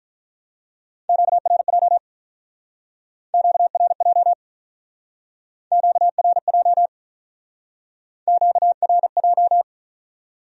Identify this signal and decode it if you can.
Morse code, heard as a keyed tone, about 46 wpm, 700 Hz. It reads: QRJ QRJ QRJ QRJ